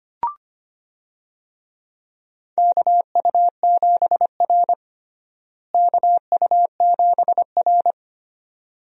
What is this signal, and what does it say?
Morse code, heard as a keyed tone, about 25 wpm, 700 Hz. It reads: KU7R KU7R
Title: USA Call Signs - 4 Characters - Repeated 2x - 25wpm